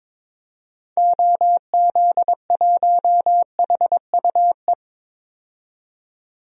Morse code, heard as a keyed tone, about 22 wpm, 700 Hz. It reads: OZ1HUE
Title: DXCC-100 Call Signs - 6 Characters - 22wpm